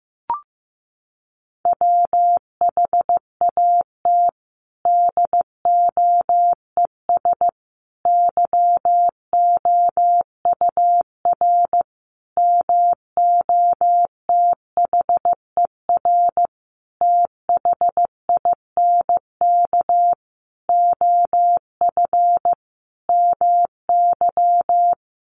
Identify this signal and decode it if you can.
Morse code, heard as a keyed tone, about 14 wpm, 700 Hz. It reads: WHAT DOES YOUR MOTHER THINK OF MY
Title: Sentences from Top 1400 Words 15wpm